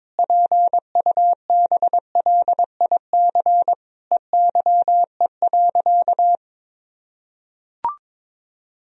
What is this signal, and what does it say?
Morse code, heard as a keyed tone, about 22 wpm, 700 Hz. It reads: PUBLIC EYE.
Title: Sentences from Top 1000 Words - Review Part 3 - 22wpm